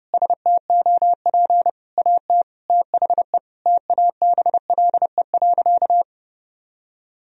Morse code, heard as a keyed tone, about 30 wpm, 700 Hz. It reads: STOP AT THE TABLE.
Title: Sentences from Top 1000 Words - Review Part 2 - 30wpm